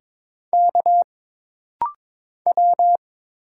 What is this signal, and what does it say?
Morse code, heard as a keyed tone, about 22 wpm, 700 Hz. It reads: K W
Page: This rapid-fire timing is geared to push your ICR (Instant Character Recognition) skills.